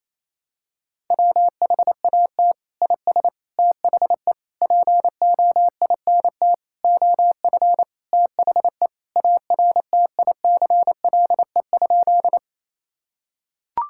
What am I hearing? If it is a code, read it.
Morse code, heard as a keyed tone, about 27 wpm, 700 Hz. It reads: WHAT IS THE POINT OF THE ARTICLE?
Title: Sentences from Top 1000 Words - Review Part 5 - 28wpm